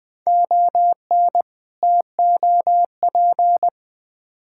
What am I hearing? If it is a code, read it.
Morse code, heard as a keyed tone, about 20 wpm, 700 Hz. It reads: ON TOP
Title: Build A Phrase - 3-Word Phrases - 20wpm